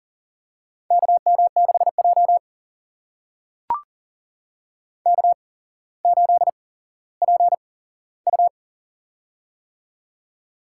Morse code, heard as a keyed tone, about 40 wpm, spaced as for 17 wpm, 700 Hz. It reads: KM6J K8PU
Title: USA Call Signs - 4 Characters - 8x Character Spacing - 40wpm